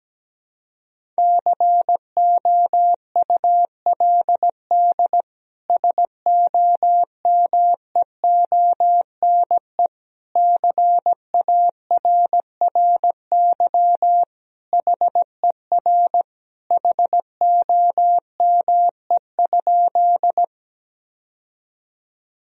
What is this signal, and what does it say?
Morse code, heard as a keyed tone, about 17 wpm, 700 Hz. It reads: COULD SOMEONE CARRY HER HOME?
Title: Sentences from Top 1000 Words - Review Part 4 - 17wpm